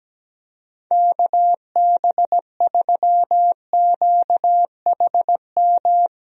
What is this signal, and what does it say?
Morse code, heard as a keyed tone, about 17 wpm, 700 Hz. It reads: KB3QHM
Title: USA Call Signs - 6 Characters - Repeated 3x - 17wpm